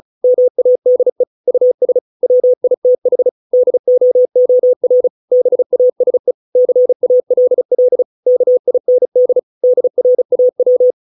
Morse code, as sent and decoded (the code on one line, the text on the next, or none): -- .- -.. / ..- ... / .-- .. - .... / -.. --- --- .-. / -... .- ... / -.-. .- .-.. .-.. / -.- .. -. -.. / -.. .-. .- .--
MADE US WITH DOOR BASE CALL KIND DRAW